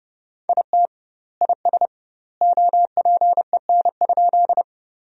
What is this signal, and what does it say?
Morse code, heard as a keyed tone, about 30 wpm, 700 Hz. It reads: IT IS OPEN?
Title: 2x Word-Spacing - Sentences from Top 300 Words - 30wpm